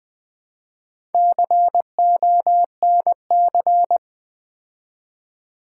Morse code, heard as a keyed tone, about 20 wpm, 700 Hz. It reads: CONC
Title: Build A Word - 2 Letters At A Time - 8-Letter Words - 20wpm